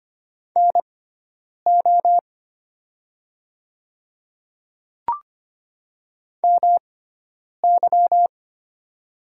Morse code, heard as a keyed tone, about 25 wpm, spaced as for 9 wpm, 700 Hz. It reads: NO MY
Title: 6x Character Spacing - Top 200 Words - 25wpm